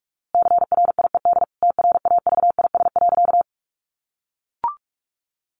Morse code, heard as a keyed tone, about 44 wpm, 700 Hz. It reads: CRIED TRAVIS.
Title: A Sound Of Thunder 45wpm